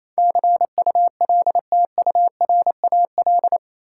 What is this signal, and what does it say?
Morse code, heard as a keyed tone, about 28 wpm, 700 Hz. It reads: CULTURAL
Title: Sentences from Top 1500 Words - New Word Drill - 28wpm